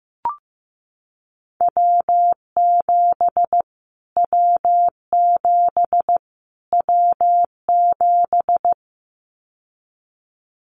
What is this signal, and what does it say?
Morse code, heard as a keyed tone, about 15 wpm, 700 Hz. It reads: W7 W7 W7
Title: Callsign Prefixes for Top 25 DXCC - Repeated 3x - 15wpm